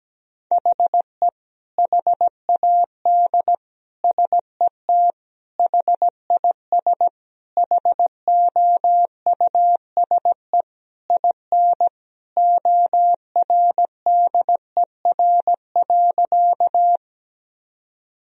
Morse code, heard as a keyed tone, about 17 wpm, 700 Hz. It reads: HE HAD SET HIS HOUSE IN ORDER.
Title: Sentences from Top 1000 Words - Review Part 2 - 17wpm